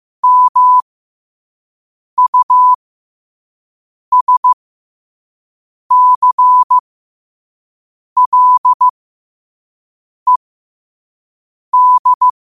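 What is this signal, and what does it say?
Morse code, heard as a keyed tone, about 15 wpm, spaced as for 5 wpm, 1000 Hz. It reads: MUSCLED
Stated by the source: Morse code quotes sent at 5 words per minute.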